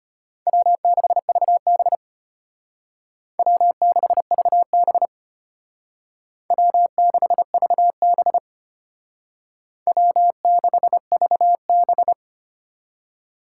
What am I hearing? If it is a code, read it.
Morse code, heard as a keyed tone, about 33 wpm, 700 Hz. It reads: W6VB W6VB W6VB W6VB